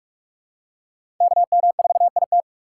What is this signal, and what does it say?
Morse code, heard as a keyed tone, about 47 wpm, 700 Hz. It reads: KM4IT
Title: USA Call Signs - 4x Character Spacing - 45wpm